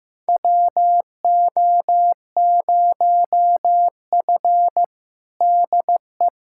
Morse code, heard as a keyed tone, about 15 wpm, 700 Hz. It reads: WO0F DE
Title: USA Callsign DE Callsign - 4 Characters - 15wpm